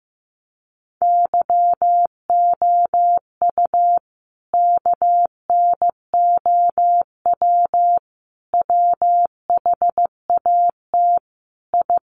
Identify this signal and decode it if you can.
Morse code, heard as a keyed tone, about 15 wpm, 700 Hz. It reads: YOU KNOW WHAT I
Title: Sets of 5 Words 15wpm